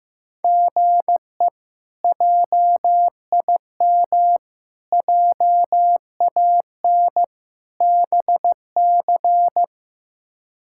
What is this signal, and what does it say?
Morse code, heard as a keyed tone, about 15 wpm, 700 Hz. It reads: GE JIM JAN BC
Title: K1USN SST QSO Practice - 15wpm